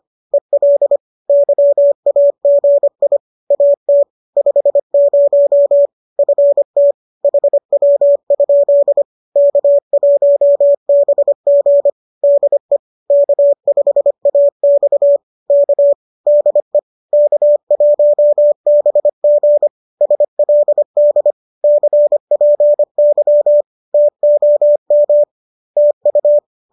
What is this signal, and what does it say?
Morse code, heard as a keyed tone, about 25 wpm, 570 Hz. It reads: EL YAGI AT 50 FT HW? K1BG DE K5AX K DE K1BG SLD CPY TOM TU